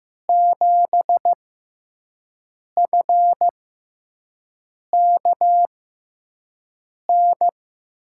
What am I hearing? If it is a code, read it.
Morse code, heard as a keyed tone, about 15 wpm, spaced as for 5 wpm, 700 Hz. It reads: 7FKN
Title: USA Call Signs - 5 Characters - 6x Character Spacing - 15wpm